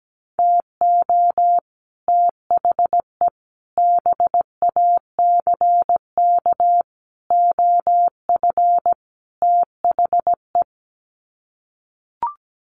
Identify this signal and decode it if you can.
Morse code, heard as a keyed tone, about 17 wpm, 700 Hz. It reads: TO THE BACK OF THE
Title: Sets of 5 Words 17wpm